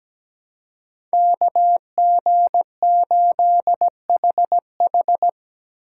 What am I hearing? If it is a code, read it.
Morse code, heard as a keyed tone, about 17 wpm, 700 Hz. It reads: KG8HH
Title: USA Call Signs - 2x Character Spacing - 17wpm